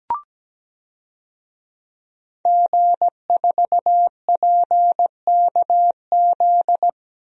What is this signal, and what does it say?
Morse code, heard as a keyed tone, about 17 wpm, 700 Hz. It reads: G4PKZ
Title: DXCC-100 Call Signs - 5 Characters - Repeated 3x - 17wpm